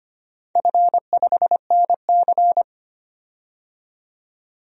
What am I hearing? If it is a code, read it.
Morse code, heard as a keyed tone, about 25 wpm, 700 Hz. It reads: F5NC
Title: DXCC-100 Call Signs - 4 Characters - 25wpm